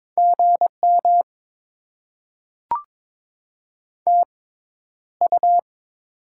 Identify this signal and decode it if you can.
Morse code, heard as a keyed tone, about 22 wpm, spaced as for 11 wpm, 700 Hz. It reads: GM TU
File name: Top 20 QSO Elements - 6x Character Spacing - 22wpm